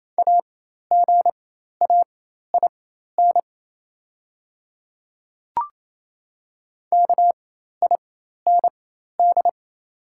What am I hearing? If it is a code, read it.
Morse code, heard as a keyed tone, about 28 wpm, spaced as for 13 wpm, 690 Hz. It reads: AGAIN KIND
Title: 4x Character Spacing - Top 200 Words - 28wpm